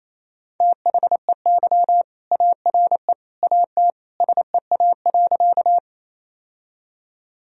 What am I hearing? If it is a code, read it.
Morse code, heard as a keyed tone, about 28 wpm, 700 Hz. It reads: THEY ARE AT SEA.
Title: Sentences from Top 1000 Words - Review Part 1 - 28wpm